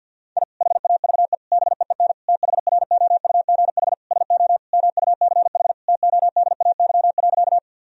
Morse code, heard as a keyed tone, about 48 wpm, 700 Hz. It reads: I HAVE BEEN THROUGH SO MUCH TODAY.
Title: Sentences from Top 1000 Words - Review Part 4 - 50wpm